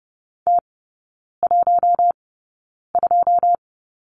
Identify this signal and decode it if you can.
Morse code, heard as a keyed tone, about 30 wpm, spaced as for 9 wpm, 700 Hz. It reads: T12
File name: 081 Lesson 14e - Number 2 - Three Characters 30wpm